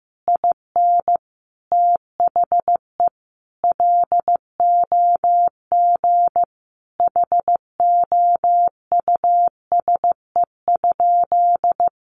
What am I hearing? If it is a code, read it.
Morse code, heard as a keyed tone, about 15 wpm, 700 Hz. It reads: IN THE LOG HOUSE?
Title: Sentences from Top 1400 Words 15wpm